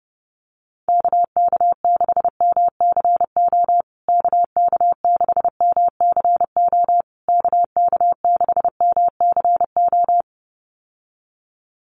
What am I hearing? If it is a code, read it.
Morse code, heard as a keyed tone, about 30 wpm, 700 Hz. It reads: KK6MCO KK6MCO KK6MCO
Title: USA Call Signs Repeated 3x 30wpm